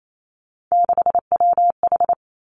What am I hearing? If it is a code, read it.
Morse code, heard as a keyed tone, about 28 wpm, 700 Hz. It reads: 6WH
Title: Three Challenging Letter-Number Combinations 28wpm